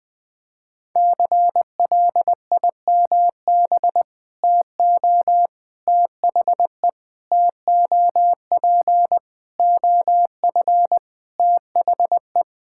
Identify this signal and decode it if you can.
Morse code, heard as a keyed tone, about 19 wpm, 700 Hz. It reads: CLIMB TO THE TOP OF THE
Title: Sentences with 4-Word Phrases - 20wpm